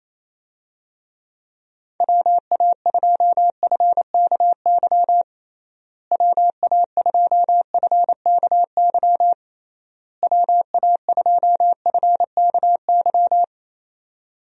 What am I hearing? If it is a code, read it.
Morse code, heard as a keyed tone, about 28 wpm, 700 Hz. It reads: WA2FKY WA2FKY WA2FKY
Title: USA Call Signs - 6 Characters - Repeated 3x - 28wpm